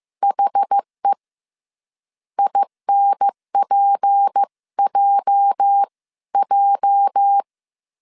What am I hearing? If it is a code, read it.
Morse code, heard as a keyed tone, about 15 wpm, 800 Hz. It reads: HE INPJJ
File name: CW / DET / 1 9 1500C S A 14wpm 3222